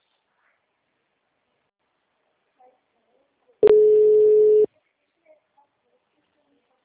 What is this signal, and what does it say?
Ringback tone (Europe, 425 Hz), with speech.
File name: out- -2002- - - .77389